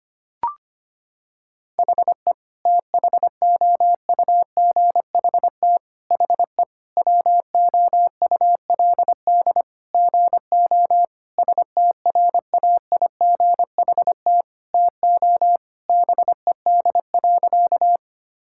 Morse code, heard as a keyed tone, about 25 wpm, 700 Hz. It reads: HE THOUGHT HE WOULD GO STRAIGHT TO BED.
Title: Sentences from Top 1000 Words - Review Part 6 - 25wpm